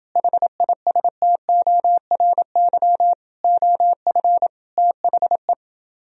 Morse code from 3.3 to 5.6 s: OF THE

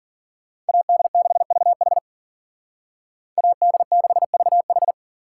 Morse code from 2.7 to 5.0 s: AD6VH

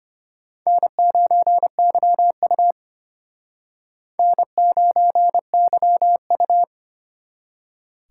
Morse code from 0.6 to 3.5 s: N9YU

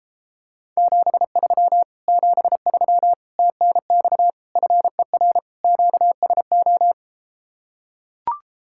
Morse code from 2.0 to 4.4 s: 73 TNX

